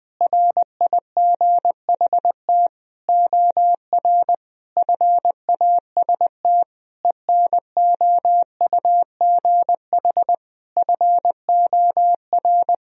Morse code, heard as a keyed tone, about 19 wpm, 700 Hz. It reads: RIGHT OR FAST ENOUGH FOR
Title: Sentences from Top 1000 Words - Review Part 2 - 20wpm